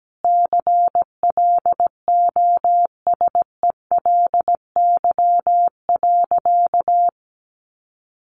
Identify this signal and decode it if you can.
Morse code, heard as a keyed tone, about 17 wpm, 700 Hz. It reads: CLOSELY.
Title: Sentences from Top 1400 Words 17wpm